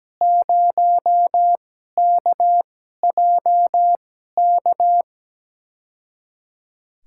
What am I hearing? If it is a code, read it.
Morse code, heard as a keyed tone, about 17 wpm, spaced as for 12 wpm, 700 Hz. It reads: 0KJK
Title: USA Call Signs - 6 Characters - 2x Character Spacing - 17wpm